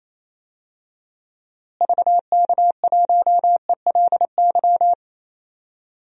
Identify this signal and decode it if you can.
Morse code, heard as a keyed tone, about 27 wpm, 700 Hz. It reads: VK1ELY
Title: DXCC-100 Call Signs - 6 Characters - Repeated 3x - 28wpm